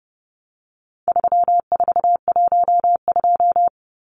Morse code, heard as a keyed tone, about 30 wpm, 700 Hz. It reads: 3412
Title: Numbers - 4-Digit 30wpm